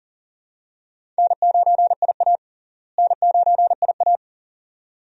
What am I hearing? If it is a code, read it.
Morse code, heard as a keyed tone, about 40 wpm, 700 Hz. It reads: N9IA N9IA
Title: USA Call Signs - 4 Characters - Repeated 2x - 40wpm